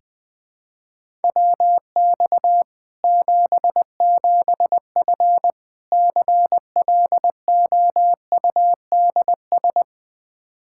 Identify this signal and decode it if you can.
Morse code, heard as a keyed tone, about 20 wpm, 700 Hz. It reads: WX 77F CLOUDS
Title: Practice QSO - Exchange 2 - Info - 20wpm